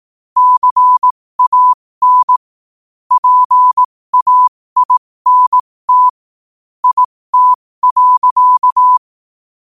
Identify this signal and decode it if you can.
Morse code, heard as a keyed tone, about 18 wpm, spaced as for 15 wpm, 1000 Hz. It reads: CAN PAINT IT.
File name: qotdfiles / 15wpm